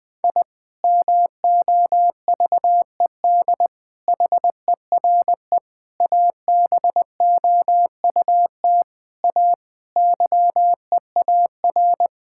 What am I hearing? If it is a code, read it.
Morse code, heard as a keyed tone, about 20 wpm, 700 Hz. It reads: I MOVED HERE ABOUT A YEAR
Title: Sentences with 7-Word Phrases - 20wpm